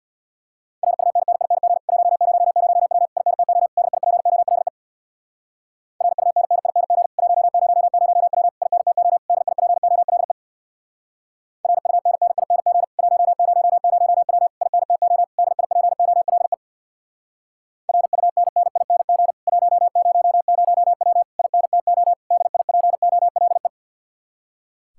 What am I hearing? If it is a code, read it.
Morse code, heard as a keyed tone, about 66 wpm, 700 Hz. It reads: RUNNING 100W INTO DIPOLE RUNNING 100W INTO DIPOLE RUNNING 100W INTO DIPOLE RUNNING 100W INTO DIPOLE